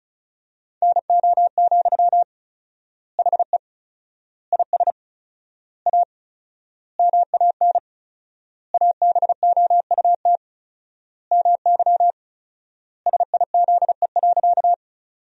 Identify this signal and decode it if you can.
Morse code, heard as a keyed tone, about 36 wpm, 700 Hz. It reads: NO, HE IS A MAN ABOUT MY SIZE.
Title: 4x Word-Spacing - Sentences from Top 400 Words - 35wpm